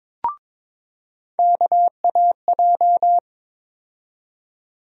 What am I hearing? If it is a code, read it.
Morse code, heard as a keyed tone, about 22 wpm, 700 Hz. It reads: KAJ
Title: USA Call Sign Postfixes - 3 Letters 22wpm